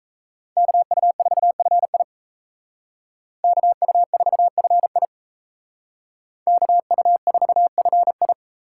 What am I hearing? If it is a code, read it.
Morse code, heard as a keyed tone, about 33 wpm, 700 Hz. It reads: KU4FI KU4FI KU4FI
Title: Speed Racing - USA 2x2 Call Signs 28wpm